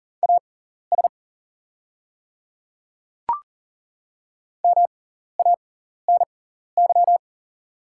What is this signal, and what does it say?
Morse code, heard as a keyed tone, about 40 wpm, spaced as for 14 wpm, 690 Hz. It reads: AS MANY
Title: 6x Character Spacing - Top 100 Words - 40wpm